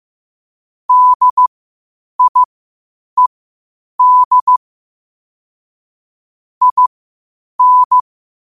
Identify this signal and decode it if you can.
Morse code, heard as a keyed tone, about 15 wpm, spaced as for 8 wpm, 1000 Hz. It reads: DIED IN
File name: qotdfiles / 8wpm